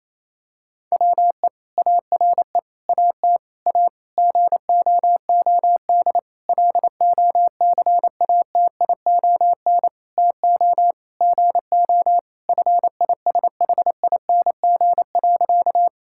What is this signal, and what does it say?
Morse code, heard as a keyed tone, about 28 wpm, 700 Hz. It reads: WE ARE AT A GOOD LOCATION TO GO FISHING.
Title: Sentences from Top 1000 Words - Review Part 6 - 28wpm